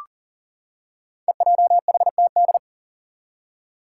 Morse code, heard as a keyed tone, about 40 wpm, 700 Hz. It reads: EJHTD
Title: Five Challenging Letter-Number Combinations 40wpm